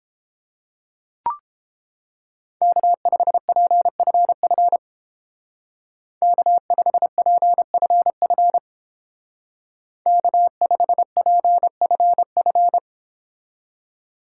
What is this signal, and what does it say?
Morse code, heard as a keyed tone, about 28 wpm, 700 Hz. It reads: K5PFF K5PFF K5PFF